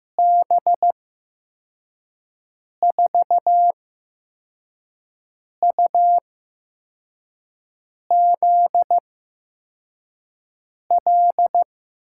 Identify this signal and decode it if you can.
Morse code, heard as a keyed tone, about 15 wpm, spaced as for 4 wpm, 700 Hz. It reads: B4UZL
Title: USA Call Signs - 6 Characters - 8x Character Spacing - 15wpm